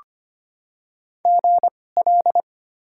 Morse code, heard as a keyed tone, about 25 wpm, spaced as for 18 wpm, 700 Hz. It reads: GL
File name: Top 90 QSO Elements - 2x Character Spacing - 25wpm